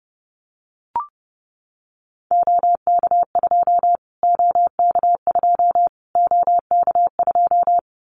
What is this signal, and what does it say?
Morse code, heard as a keyed tone, about 30 wpm, 700 Hz. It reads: OK2 OK2 OK2
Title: Callsign Prefixes for Top 25 DXCC - Repeated 3x - 30wpm